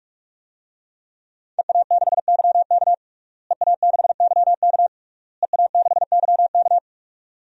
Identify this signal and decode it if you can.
Morse code, heard as a keyed tone, about 41 wpm, 700 Hz. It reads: EA6YK EA6YK EA6YK
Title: DXCC-100 Call Signs - 5 Characters - Repeated 3x - 45wpm